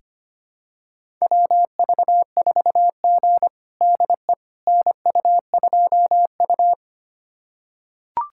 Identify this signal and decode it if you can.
Morse code, heard as a keyed tone, about 25 wpm, 700 Hz. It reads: WV4G DE NU2U
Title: USA Callsign DE Callsign - 4 Characters - 25wpm